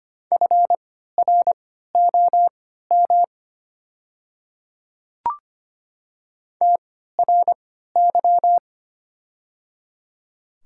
Morse code, heard as a keyed tone, about 25 wpm, spaced as for 14 wpm, 690 Hz. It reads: FROM TRY
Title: 3x Character Spacing - Top 200 Words - 25wpm